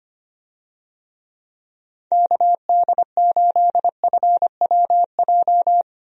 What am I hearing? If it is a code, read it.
Morse code, heard as a keyed tone, about 25 wpm, 700 Hz. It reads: KD8FWJ
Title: USA Call Signs - 6 Characters - Repeated 3x - 25wpm